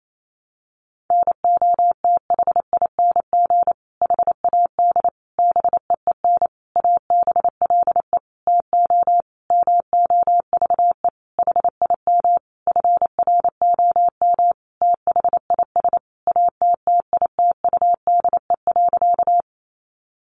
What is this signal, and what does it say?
Morse code, heard as a keyed tone, about 28 wpm, 700 Hz. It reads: NOTHING HAD BEEN ABLE TO MOVE HIM FROM THIS ATTITUDE.
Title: Sentences from Top 1300 Words 28wpm